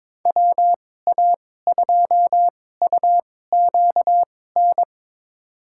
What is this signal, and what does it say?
Morse code, heard as a keyed tone, about 22 wpm, spaced as for 16 wpm, 700 Hz. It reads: WA2UQN
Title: USA Call Signs - 6 Characters - 2x Character Spacing - 22wpm